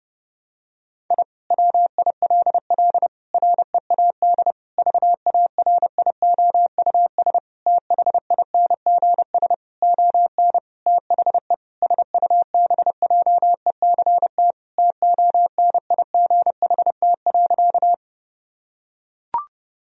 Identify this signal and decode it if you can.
Morse code, heard as a keyed tone, about 30 wpm, 700 Hz. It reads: I WILL READ VARIOUS THINGS ON THE SUBJECT TONIGHT.
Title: Sentences from Top 1000 Words - Review Part 4 - 30wpm